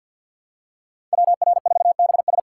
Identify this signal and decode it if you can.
Morse code, heard as a keyed tone, about 48 wpm, 700 Hz. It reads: WR4BS
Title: USA Call Signs - 5 Characters - 6x Character Spacing - 50wpm